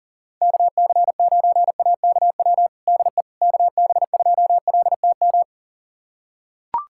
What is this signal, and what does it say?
Morse code, heard as a keyed tone, about 37 wpm, 700 Hz. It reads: KC9AKW DE KB2LTM
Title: USA Callsign DE Callsign 40wpm